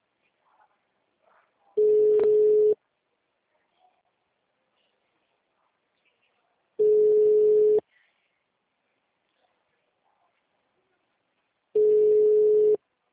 Ringback tone (Europe, 425 Hz).